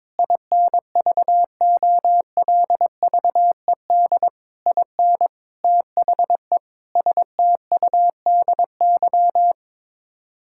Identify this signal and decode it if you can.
Morse code, heard as a keyed tone, about 22 wpm, 700 Hz. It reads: INVOLVED IN THE STUDY
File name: practice / 22wpm / Sets of 7 Words - 22wpm